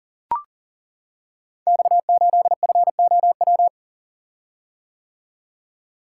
Morse code, heard as a keyed tone, about 40 wpm, 700 Hz. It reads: X8FOW
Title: Code Groups - 5 Letter:Numbers - 40wpm